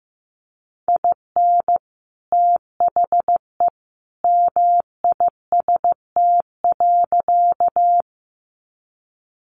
Morse code, heard as a keyed tone, about 15 wpm, 700 Hz. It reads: IN THE MIST.
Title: A Sound Of Thunder 15wpm